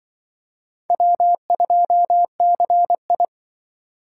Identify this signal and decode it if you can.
Morse code, heard as a keyed tone, about 25 wpm, 700 Hz. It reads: W2CI
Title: Speed Racing - USA 1x2 Call Signs 20wpm